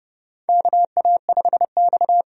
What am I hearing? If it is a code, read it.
Morse code, heard as a keyed tone, about 32 wpm, 700 Hz. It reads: KA5X
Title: USA Call Signs - 4 Characters - 30wpm